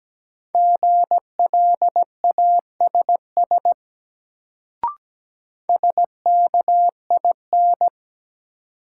Morse code, heard as a keyed tone, about 17 wpm, 700 Hz. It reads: GLASS SKIN
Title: Sentences from Top 1300 Words - New Word Drill Alt - 17wpm